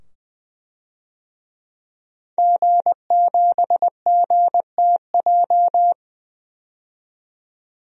Morse code, heard as a keyed tone, about 20 wpm, 700 Hz. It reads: G7GTJ
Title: DXCC-100 Call Signs - 5 Characters - 20wpm